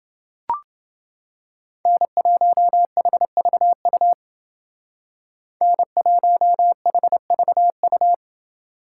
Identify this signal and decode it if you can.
Morse code, heard as a keyed tone, about 28 wpm, 700 Hz. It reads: N1HVU N1HVU